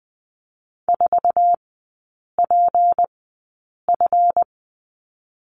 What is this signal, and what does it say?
Morse code, heard as a keyed tone, about 20 wpm, spaced as for 8 wpm, 700 Hz. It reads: 4PF
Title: 166 Lesson 27e - Number 7 - Three Characters 20wpm